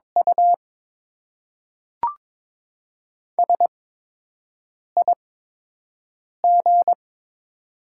Morse code, heard as a keyed tone, about 22 wpm, spaced as for 6 wpm, 700 Hz. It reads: U SIG
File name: Top 90 QSO Elements - 8x Character Spacing - 22wpm